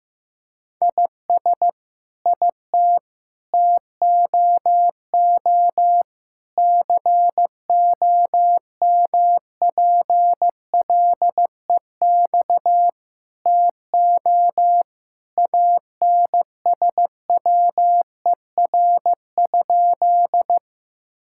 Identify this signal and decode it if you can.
Morse code, heard as a keyed tone, about 15 wpm, 700 Hz. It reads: IS IT TOO COMPLEX TO ANSWER?
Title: Sentences from Top 1000 Words - Review Part 6 - 15wpm